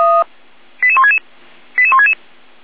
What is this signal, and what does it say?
Five-tone selective call (ZVEI), heard as quick runs of five tones, tones of about 70 ms. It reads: 88166 88166